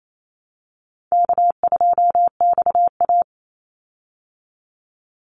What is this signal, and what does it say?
Morse code, heard as a keyed tone, about 28 wpm, 700 Hz. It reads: K2XA
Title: Easy Call Signs Repeated 3x - 28wpm